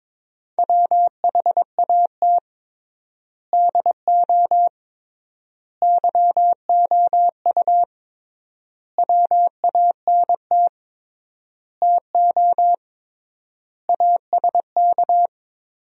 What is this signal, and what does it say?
Morse code, heard as a keyed tone, about 22 wpm, 700 Hz. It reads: WHAT DO YOU WANT TO ASK
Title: 3x Word-Spacing - Sentences from Top 200 Words - 22wpm